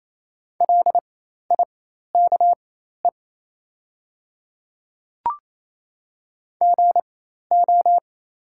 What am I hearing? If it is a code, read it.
Morse code, heard as a keyed tone, about 28 wpm, spaced as for 13 wpm, 700 Hz. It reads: LIKE GO
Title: 4x Character Spacing - Top 100 Words - 28wpm